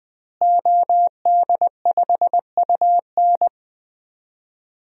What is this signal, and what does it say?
Morse code, heard as a keyed tone, about 19 wpm, 700 Hz. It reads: OD5UN
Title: DXCC-100 Call Signs - 5 Characters - Repeated 3x - 20wpm